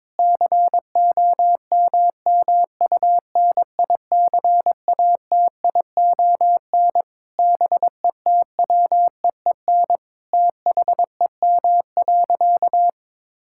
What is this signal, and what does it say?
Morse code, heard as a keyed tone, about 22 wpm, 700 Hz. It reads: COMMUNICATION BETWEEN THEM.
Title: Sentences from Top 1000 Words - Review Part 6 - 22wpm